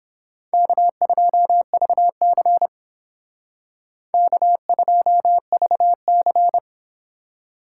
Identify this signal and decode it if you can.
Morse code, heard as a keyed tone, about 28 wpm, 700 Hz. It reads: K2VC K2VC